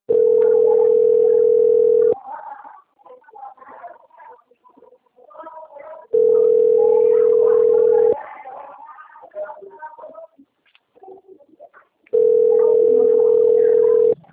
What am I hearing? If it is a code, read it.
Ringback tone (North America), 440 + 480 Hz.